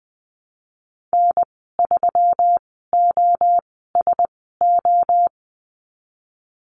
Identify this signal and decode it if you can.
Morse code, heard as a keyed tone, about 20 wpm, spaced as for 15 wpm, 700 Hz. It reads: N3OSO
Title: 154 Lesson 25g - Letter p - Call Signs 20wpm